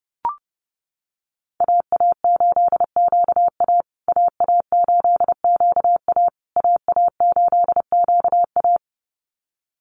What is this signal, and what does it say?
Morse code, heard as a keyed tone, about 30 wpm, 700 Hz. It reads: AA8QA AA8QA AA8QA
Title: USA Call Signs Repeated 3x 30wpm